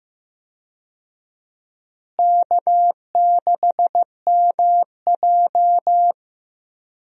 Morse code, heard as a keyed tone, about 15 wpm, 700 Hz. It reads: K6MJ
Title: USA Call Signs - 4 Characters - Repeated 3x - 15wpm